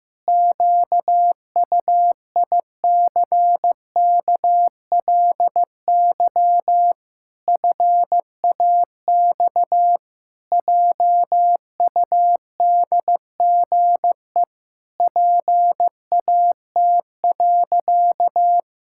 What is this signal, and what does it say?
Morse code, heard as a keyed tone, about 15 wpm, 700 Hz. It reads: QUICKLY FAX JUDGE PAT.
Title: Pangram Challenge v2 - 15wpm